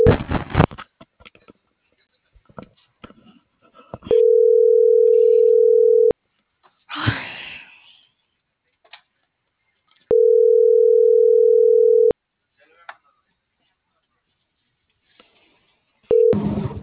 Ringback tone (North America), 440 + 480 Hz.